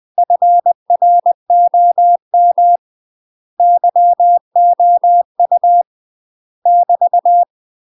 Morse code, heard as a keyed tone, about 20 wpm, 700 Hz. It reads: FROM YOU <BT>
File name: UK MASTER QSOs TRAINING / 20WPM / UK MASTER QSOs 0004